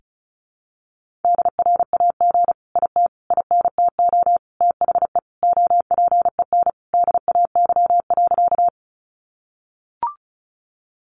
Morse code, heard as a keyed tone, about 34 wpm, 700 Hz. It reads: DRAG IT INTO THE OPEN DAY.
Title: Sentences from Top 1500 Words 35wpm